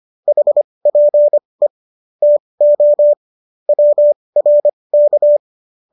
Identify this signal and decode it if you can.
Morse code, heard as a keyed tone, about 25 wpm, spaced as for 20 wpm, 590 Hz. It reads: HPE TO WRK